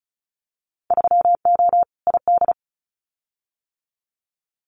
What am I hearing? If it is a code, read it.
Morse code, heard as a keyed tone, about 36 wpm, 700 Hz. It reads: 3O ID
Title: Winter Field Day Exchanges Encore 35wpm